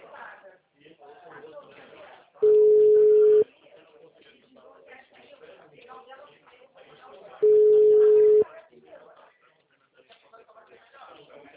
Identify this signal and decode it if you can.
Ringback tone (Europe, 425 Hz), with speech.